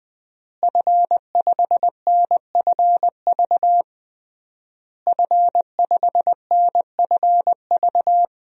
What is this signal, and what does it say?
Morse code, heard as a keyed tone, about 21 wpm, 700 Hz. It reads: F5NFV F5NFV
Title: DXCC-100 Call Signs - 5 Characters - Repeated 3x - 20wpm